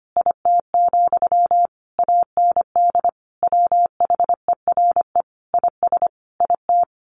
Morse code, heard as a keyed tone, about 25 wpm, 700 Hz. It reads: IT, AND WHERE IS IT
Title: Sentences from Top 300 Words 25wpm